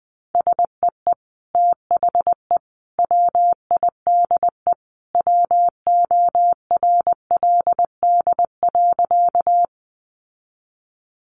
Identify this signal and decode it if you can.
Morse code, heard as a keyed tone, about 20 wpm, 700 Hz. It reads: SEE THE WIDE WORLD.
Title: Sentences from Top 900 Words 20wpm